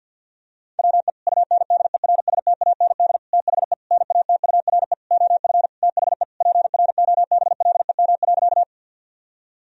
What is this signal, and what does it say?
Morse code, heard as a keyed tone, about 52 wpm, 700 Hz. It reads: WE UNDERSTAND THE NATURE OF THE PROBLEM.
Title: Sentences from Top 1000 Words - Review Part 2 - 50wpm